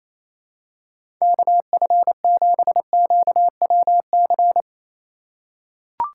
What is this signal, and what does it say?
Morse code, heard as a keyed tone, about 28 wpm, 700 Hz. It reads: KF7QWC
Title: USA Call Signs - 6 Characters - 3x Character Spacing - 28wpm